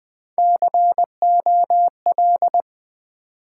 Morse code, heard as a keyed tone, about 20 wpm, 700 Hz. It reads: COL
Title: Build A Word - 3 Letters At A Time - 8-Letter Words - 20wpm